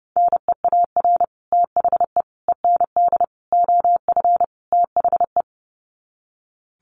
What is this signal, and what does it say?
Morse code, heard as a keyed tone, about 30 wpm, 700 Hz. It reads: NEAR THE END OF THE
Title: Sets of 5 Words Encore 30wpm